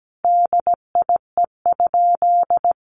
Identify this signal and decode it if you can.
Morse code, heard as a keyed tone, about 17 wpm, 700 Hz. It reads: DIE?
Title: Sentences from Top 1300 Words 17wpm